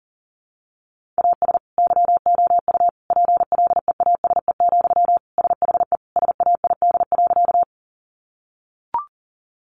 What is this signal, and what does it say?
Morse code, heard as a keyed tone, about 40 wpm, 700 Hz. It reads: AS YOU PLEASE, SHE SAID.
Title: Sentences from Top 1500 Words 40wpm